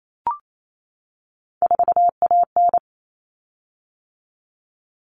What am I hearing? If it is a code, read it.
Morse code, heard as a keyed tone, about 28 wpm, 700 Hz. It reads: 4AN
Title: Common Cut Numbers - 3-Digit 28wpm